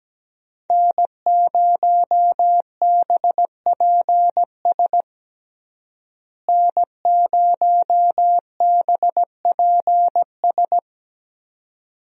Morse code, heard as a keyed tone, about 17 wpm, 700 Hz. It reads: N0BPS N0BPS
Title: USA Call Signs - 5 Characters - Repeated 2x - 17wpm